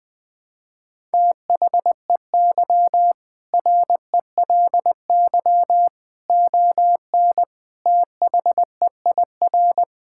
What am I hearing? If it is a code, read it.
Morse code, heard as a keyed tone, about 20 wpm, 700 Hz. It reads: THEY RELY ON THEIR
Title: Sentences with 2-Word Phrases - 20wpm